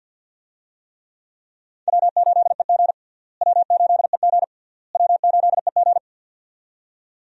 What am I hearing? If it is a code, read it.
Morse code, heard as a keyed tone, about 53 wpm, 700 Hz. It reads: W8EG W8EG W8EG
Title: USA Call Signs - 4 Characters - Repeated 3x - 50wpm